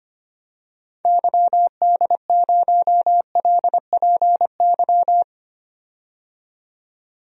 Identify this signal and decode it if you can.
Morse code, heard as a keyed tone, about 25 wpm, 700 Hz. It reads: YD0LPY
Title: DXCC-100 Call Signs - 6 Characters - 25wpm